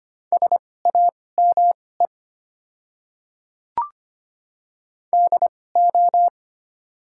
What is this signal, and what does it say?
Morse code, heard as a keyed tone, about 25 wpm, spaced as for 18 wpm, 690 Hz. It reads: SAME DO
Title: 2x Character Spacing - Top 200 Words - 25wpm